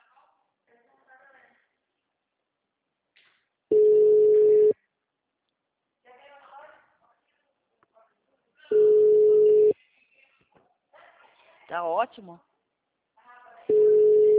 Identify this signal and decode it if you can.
Ringback tone (Europe, 425 Hz), with speech.
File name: out- -2048- - - .95861